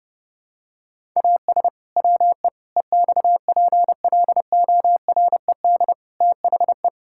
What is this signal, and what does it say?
Morse code, heard as a keyed tone, about 28 wpm, 700 Hz. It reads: AS WE EXPLORED THE
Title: Pangram Challenge v2 - 30wpm